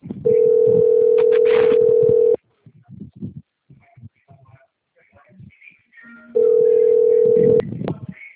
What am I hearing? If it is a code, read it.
Ringback tone (North America), 440 + 480 Hz.
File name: out- - - - - .50547